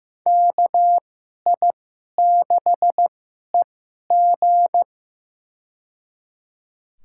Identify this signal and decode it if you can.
Morse code, heard as a keyed tone, about 15 wpm, spaced as for 11 wpm, 700 Hz. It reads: KI6EG
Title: USA Call Signs - 2x Character Spacing - 15wpm